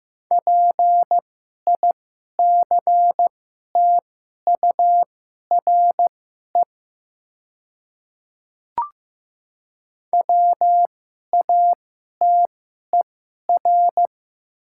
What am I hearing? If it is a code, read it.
Morse code, heard as a keyed tone, about 15 wpm, spaced as for 11 wpm, 700 Hz. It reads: PICTURE WATER
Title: 2x Character Spacing - Top 500 Words - 15wpm